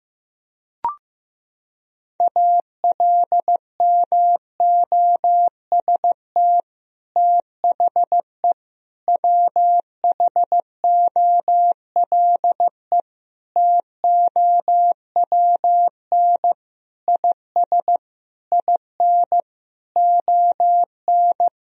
Morse code, heard as a keyed tone, about 15 wpm, 700 Hz. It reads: ALMOST THE WHOLE TOWN IS IN ON
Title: Sentences from Top 1000 Words - Review Part 4 - 15wpm